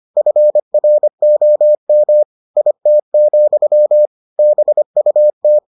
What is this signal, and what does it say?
Morse code, heard as a keyed tone, about 25 wpm, 600 Hz. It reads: FROM IT, BUT